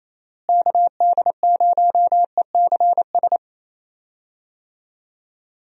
Morse code, heard as a keyed tone, about 28 wpm, 700 Hz. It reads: KD0ECS
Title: USA Call Signs - 6 Characters - Repeated 3x - 28wpm